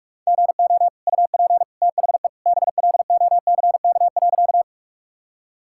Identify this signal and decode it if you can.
Morse code, heard as a keyed tone, about 46 wpm, 700 Hz. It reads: GO UP THE BLOCK.
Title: Sentences from Top 1600 Words 45wpm